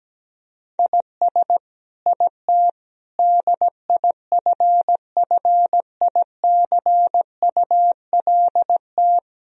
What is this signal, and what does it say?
Morse code, heard as a keyed tone, about 17 wpm, 700 Hz. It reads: IS IT DIFFICULT
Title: Sentences from Top 1000 Words - Review Part 3 - 17wpm